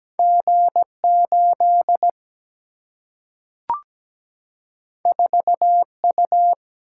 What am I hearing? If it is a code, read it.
Morse code, heard as a keyed tone, about 17 wpm, 700 Hz. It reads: G8 4U
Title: Single-Letter Single-Number 17wpm